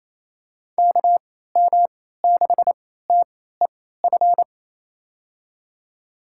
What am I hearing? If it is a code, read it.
Morse code, heard as a keyed tone, about 27 wpm, spaced as for 16 wpm, 700 Hz. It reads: KM6TEF
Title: USA Call Signs - 6 Characters - 3x Character Spacing - 28wpm